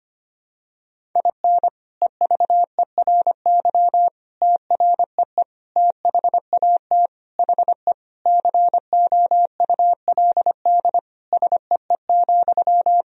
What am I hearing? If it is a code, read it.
Morse code, heard as a keyed tone, about 25 wpm, 700 Hz. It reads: IN EVERY TREE THAT HE COULD SEE,
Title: Sentences from Top 1000 Words - Review Part 1 - 25wpm